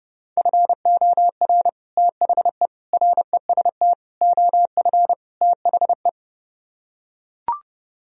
Morse code, heard as a keyed tone, about 30 wpm, 700 Hz. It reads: FOR THE REST OF THE